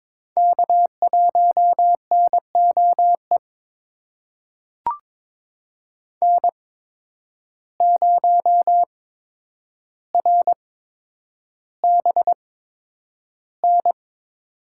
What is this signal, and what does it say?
Morse code, heard as a keyed tone, about 21 wpm, spaced as for 9 wpm, 700 Hz. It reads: K1NOE N0RBN
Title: USA Call Signs - 5 Characters - 8x Character Spacing - 22wpm